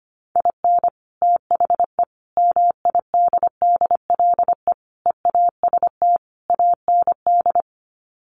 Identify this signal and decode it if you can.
Morse code, heard as a keyed tone, about 25 wpm, 700 Hz. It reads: IN THE MIDDLE EAST AND
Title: Sets of 5 Words Encore 25wpm